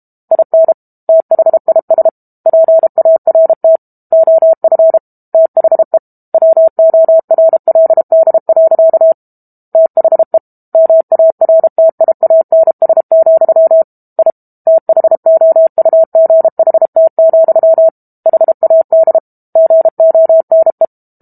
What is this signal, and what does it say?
Morse code, heard as a keyed tone, about 32 wpm, 650 Hz. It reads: IN THIS PART OF THE WORLD. THE MARTIANS, I THOUGHT, HAD GONE
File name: Book Two / 06-Chapter 6 - 32 WPM